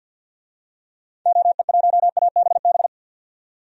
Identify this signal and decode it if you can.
Morse code, heard as a keyed tone, about 48 wpm, 700 Hz. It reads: OE1ABB
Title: DXCC-100 Call Signs - 6 Characters - 50wpm